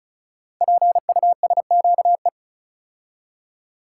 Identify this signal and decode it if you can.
Morse code, heard as a keyed tone, about 36 wpm, 690 Hz. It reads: PUSQE
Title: Code Groups - 5 Letters - 35wpm